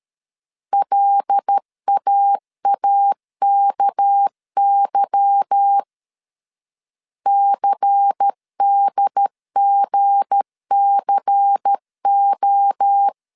Morse code, heard as a keyed tone, about 13 wpm, 800 Hz. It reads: LAAKY CDGCO
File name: CW / DET / 1 9 1500C S A 12wpm 3800